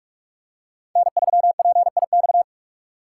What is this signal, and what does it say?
Morse code, heard as a keyed tone, about 46 wpm, 700 Hz. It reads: N3PIX
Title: USA Call Signs - 5 Characters - 4x Character Spacing - 45wpm